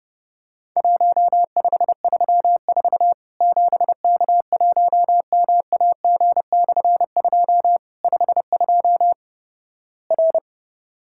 Morse code, heard as a keyed tone, about 30 wpm, 700 Hz. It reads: 1534 7K1MAG/2 52 R
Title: Worked All Europe DX Contest - QTC Exchange - 30wpm